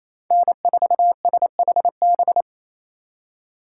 Morse code, heard as a keyed tone, about 28 wpm, 700 Hz. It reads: N4SHB
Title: USA Call Signs - 5 Characters - 4x Character Spacing - 28wpm